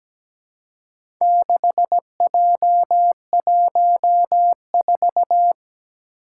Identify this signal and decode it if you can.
Morse code, heard as a keyed tone, about 17 wpm, 700 Hz. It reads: 6J14
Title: Four Challenging Letter-Number Combinations 17wpm